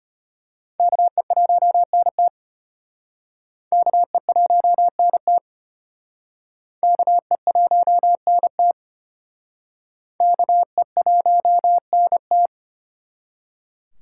Morse code, heard as a keyed tone, about 33 wpm, 700 Hz. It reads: KE1NT KE1NT KE1NT KE1NT